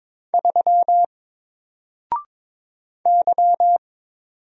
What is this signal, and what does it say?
Morse code, heard as a keyed tone, about 22 wpm, 700 Hz. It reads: 3 Y
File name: Single-Letter-Number-22wpm